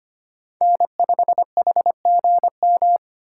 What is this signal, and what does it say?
Morse code, heard as a keyed tone, about 25 wpm, 700 Hz. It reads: N5HGM
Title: USA Call Signs - 5 Characters - 3x Character Spacing - 25wpm